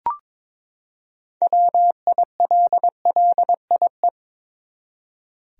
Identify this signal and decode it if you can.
Morse code, heard as a keyed tone, about 21 wpm, 700 Hz. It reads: WILLIE
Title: USA Common Names - 22wpm